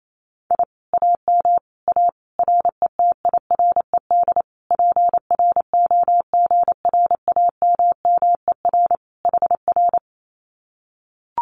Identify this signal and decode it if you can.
Morse code, heard as a keyed tone, about 28 wpm, 700 Hz. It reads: I AM A RETIRED PROGRAMMER HR
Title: Common QSO Phrases Encore 28wpm